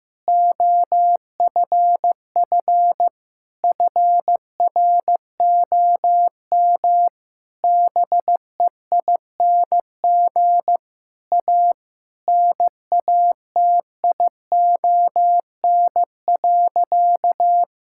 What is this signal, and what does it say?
Morse code, heard as a keyed tone, about 16 wpm, 700 Hz. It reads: OFF FROM BEING A NATION.
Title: Sentences from Top 1000 Words - Review Part 6 - 15wpm